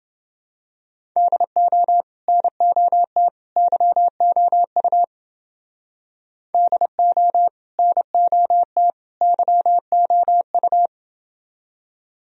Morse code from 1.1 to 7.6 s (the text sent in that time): DO NOT YOU DO